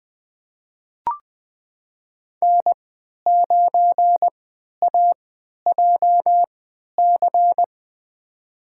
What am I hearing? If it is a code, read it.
Morse code, heard as a keyed tone, about 20 wpm, spaced as for 11 wpm, 700 Hz. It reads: N9AJC
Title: USA Call Signs - 5 Characters - 3x Character Spacing - 20wpm